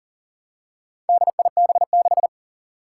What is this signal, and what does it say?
Morse code, heard as a keyed tone, about 40 wpm, 700 Hz. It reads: DIB6
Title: Four Challenging Letter-Number Combinations 40wpm